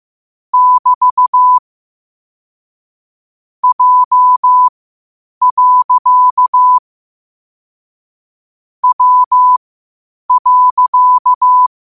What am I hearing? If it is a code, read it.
Morse code, heard as a keyed tone, about 15 wpm, spaced as for 8 wpm, 1000 Hz. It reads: <BT> J. W.
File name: qotdfiles / 8wpm